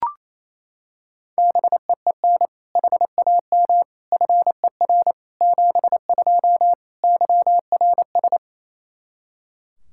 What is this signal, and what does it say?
Morse code, heard as a keyed tone, about 28 wpm, 700 Hz. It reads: BEEN HAM FER 72 YRS
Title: Practice QSO - Exchange 3 - More Info - 28wpm